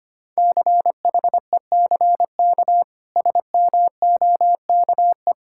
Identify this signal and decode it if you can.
Morse code, heard as a keyed tone, about 25 wpm, 700 Hz. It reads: CHECK SMOKE
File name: Easy Sentences with Hints - 25wpm